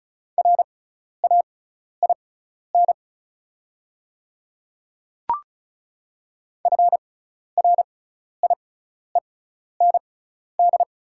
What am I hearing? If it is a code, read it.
Morse code, heard as a keyed tone, about 36 wpm, spaced as for 12 wpm, 700 Hz. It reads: RAIN FRIEND
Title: 6x Character Spacing - Top 300 Words - 35wpm